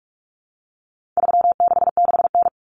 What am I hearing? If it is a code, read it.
Morse code, heard as a keyed tone, about 44 wpm, 700 Hz. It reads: 366N
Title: Common Cut Numbers - 4-Digit 45wpm